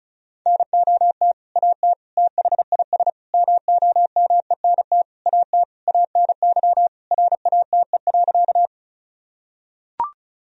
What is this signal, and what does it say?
Morse code, heard as a keyed tone, about 36 wpm, 700 Hz. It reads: NOT AT THIS MOMENT AT ANY RATE.
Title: Sentences from Top 1000 Words - Review Part 4 - 35wpm